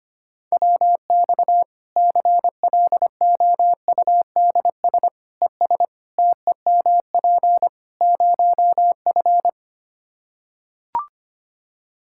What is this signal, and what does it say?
Morse code, heard as a keyed tone, about 25 wpm, 700 Hz. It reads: WX CLOUDS ES TEMP 0F
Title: Practice QSO - Exchange 2 - Info - 25wpm